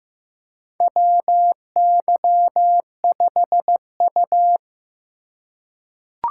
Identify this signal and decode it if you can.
Morse code, heard as a keyed tone, about 15 wpm, 700 Hz. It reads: WY5U
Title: USA Call Signs - 4 Characters - 4x Character Spacing - 15wpm